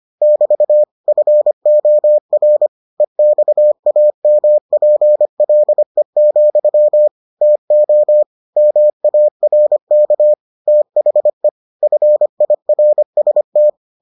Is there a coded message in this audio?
Yes: Morse code from 0.2 to 13.7 s, heard as a keyed tone, about 25 wpm, 600 Hz. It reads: <BT> FOR EXAMPLE, TO MARK THE FIRST